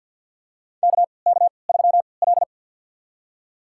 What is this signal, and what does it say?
Morse code, heard as a keyed tone, about 52 wpm, spaced as for 29 wpm, 700 Hz. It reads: KK3L
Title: USA Call Signs - 4 Characters - 3x Character Spacing - 50wpm